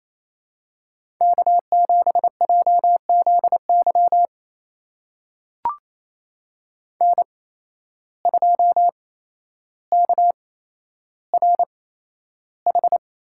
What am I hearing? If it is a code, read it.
Morse code, heard as a keyed tone, about 28 wpm, spaced as for 12 wpm, 700 Hz. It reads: K7JZY N2KRH
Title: USA Call Signs - 5 Characters - 8x Character Spacing - 28wpm